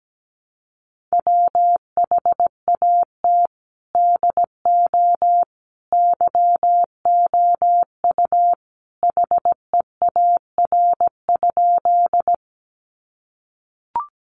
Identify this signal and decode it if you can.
Morse code, heard as a keyed tone, about 17 wpm, 700 Hz. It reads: WHAT DO YOU HEAR?
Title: Sentences from Top 300 Words 17wpm